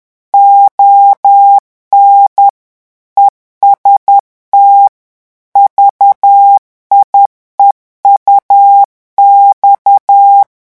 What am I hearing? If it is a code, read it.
Morse code, heard as a keyed tone, about 10 wpm, 800 Hz. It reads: ON EST VIEUX
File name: revision9 citations humoristiques vitesse 10 mots minute